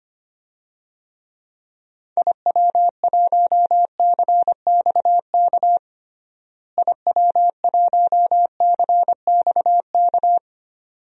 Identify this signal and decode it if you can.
Morse code, heard as a keyed tone, about 25 wpm, 700 Hz. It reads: IW1CXK IW1CXK
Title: DXCC-100 Call Signs - 6 Characters - Repeated 2x - 25wpm